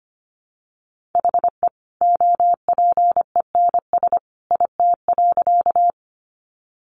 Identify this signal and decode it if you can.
Morse code, heard as a keyed tone, about 24 wpm, 700 Hz. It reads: HE OPENS IT.
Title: Additional Easy Phrases 25wpm